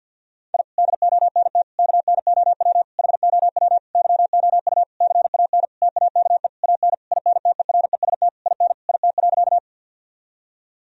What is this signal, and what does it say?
Morse code, heard as a keyed tone, about 51 wpm, 700 Hz. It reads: I DONT KNOW HOW YOU CAN TAKE AN INTEREST IN IT.
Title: Sentences from Top 1000 Words - Review Part 2 - 50wpm